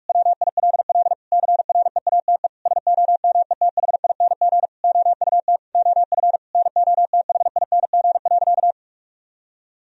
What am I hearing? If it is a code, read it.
Morse code, heard as a keyed tone, about 46 wpm, 700 Hz. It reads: WILL CREATE SOMETHING OUT OF NOTHING.
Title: Sentences from Top 1000 Words - Review Part 3 - 45wpm